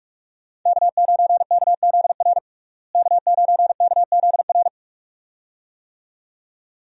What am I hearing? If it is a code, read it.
Morse code, heard as a keyed tone, about 42 wpm, 700 Hz. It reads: K9KZR K9KZR
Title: USA Call Signs - 5 Characters - Repeated 2x - 45wpm